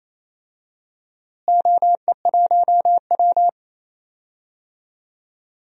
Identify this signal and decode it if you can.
Morse code, heard as a keyed tone, about 28 wpm, 700 Hz. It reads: OE1W
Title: DXCC-100 Call Signs - 4 Characters - Repeated 2x - 28wpm